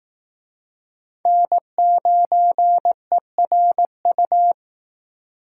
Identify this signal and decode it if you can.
Morse code, heard as a keyed tone, about 18 wpm, 700 Hz. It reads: N9ERU
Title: Speed Racing - USA 1x3 Call Signs 15wpm